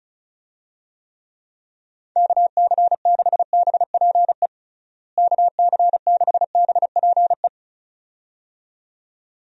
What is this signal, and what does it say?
Morse code, heard as a keyed tone, about 36 wpm, 700 Hz. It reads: KC6BPE KC6BPE
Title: USA Call Signs - 6 Characters - Repeated 2x - 35wpm